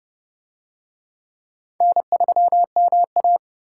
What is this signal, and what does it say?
Morse code, heard as a keyed tone, about 29 wpm, 700 Hz. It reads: N3MA